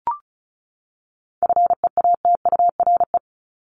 Morse code, heard as a keyed tone, about 34 wpm, 700 Hz. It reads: FEATURE
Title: Top 751-1000 Words 35wpm